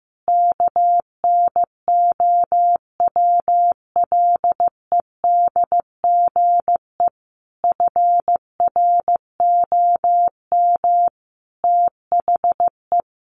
Morse code, heard as a keyed tone, about 15 wpm, 700 Hz. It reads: KNOWLEDGE FROM THE
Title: Sentences from Top 1300 Words 15wpm